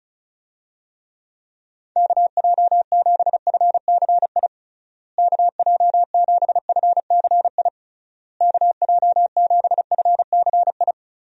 Morse code, heard as a keyed tone, about 34 wpm, 700 Hz. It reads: KJ7FCI KJ7FCI KJ7FCI
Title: USA Call Signs - 6 Characters - Repeated 3x - 35wpm